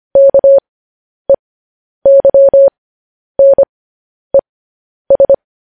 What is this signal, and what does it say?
Morse code, heard as a keyed tone, about 25 wpm, spaced as for 10 wpm, 550 Hz. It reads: KEYNES
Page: news headlines in morse code at 10 wpm